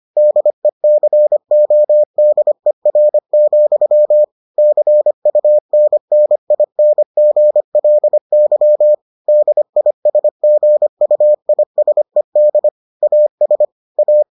Morse code, heard as a keyed tone, about 25 wpm, 600 Hz. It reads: DECODER, CUNNINGLY DISGUISED AS A